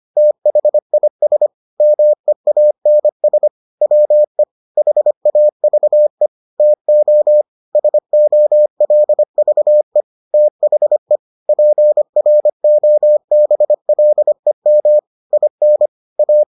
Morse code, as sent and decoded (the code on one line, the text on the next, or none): - .... .. ... / -- . .- -. ... / .-- . / .... .- ...- . / - --- / ... --- .-.. ...- . / - .... / .--. .-. --- -... .-.. -- / .. -. / .-
THIS MEANS WE HAVE TO SOLVE THE PROBLEM IN A